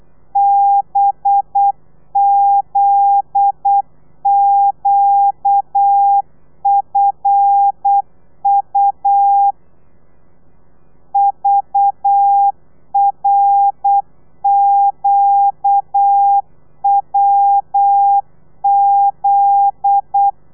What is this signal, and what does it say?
Morse code, heard as a keyed tone, about 8 wpm, 800 Hz. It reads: BZQFU VRQWZ
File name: telegrafia / i2tfj / Lez9 35